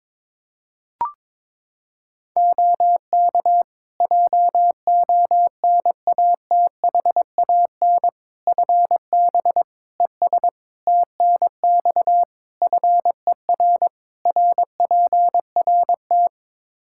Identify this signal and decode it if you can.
Morse code, heard as a keyed tone, about 22 wpm, 700 Hz. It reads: OK JONATHAN FB ES TNX FER RPRT
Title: Practice QSO - Exchange 2 - Info - 22wpm